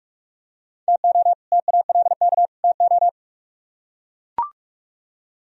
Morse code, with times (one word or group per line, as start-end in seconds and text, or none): 0.9-1.3 TO
1.5-2.5 TALK
2.6-3.1 TO